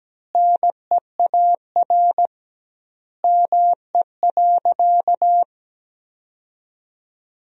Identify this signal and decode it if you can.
Morse code, heard as a keyed tone, about 17 wpm, 700 Hz. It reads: NEAR ME.
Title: 2x Word-Spacing - Sentences from Top 300 Words - 17wpm